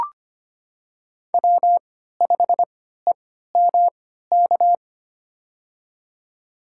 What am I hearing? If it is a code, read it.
Morse code, heard as a keyed tone, about 25 wpm, spaced as for 14 wpm, 700 Hz. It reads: W5EMK
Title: USA Call Signs - 5 Characters - 3x Character Spacing - 25wpm